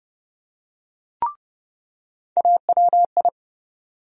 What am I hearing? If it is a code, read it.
Morse code, heard as a keyed tone, about 30 wpm, 700 Hz. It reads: AWI
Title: USA Call Sign Postfixes - 3 Letters 30wpm